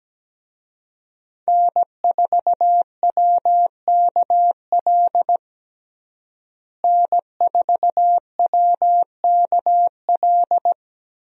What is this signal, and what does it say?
Morse code, heard as a keyed tone, about 17 wpm, 700 Hz. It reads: N4WKL N4WKL
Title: USA Call Signs - 5 Characters - Repeated 3x - 17wpm